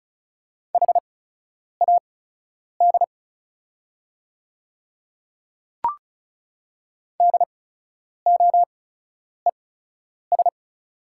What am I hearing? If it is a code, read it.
Morse code, heard as a keyed tone, about 36 wpm, spaced as for 10 wpm, 690 Hz. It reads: HAD DOES
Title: 8x Character Spacing - Top 200 Words - 35wpm